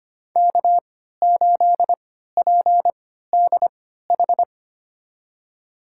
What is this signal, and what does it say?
Morse code, heard as a keyed tone, about 25 wpm, spaced as for 14 wpm, 700 Hz. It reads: K8PDH
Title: USA Call Signs - 5 Characters - 3x Character Spacing - 25wpm